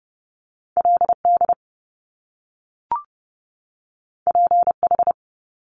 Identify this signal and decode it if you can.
Morse code, heard as a keyed tone, about 30 wpm, 700 Hz. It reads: L D P H
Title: Top Two Letter Combinations in Words 30wpm